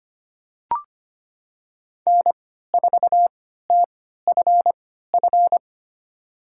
Morse code, heard as a keyed tone, about 25 wpm, spaced as for 14 wpm, 700 Hz. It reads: N4TFF
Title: USA Call Signs - 3x Character Spacing - 25wpm